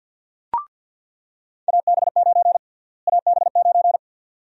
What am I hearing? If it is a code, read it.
Morse code, heard as a keyed tone, about 47 wpm, 700 Hz. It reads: AB9 AB9
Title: USA Call Sign Prefixes - Repeated 2x - 50wpm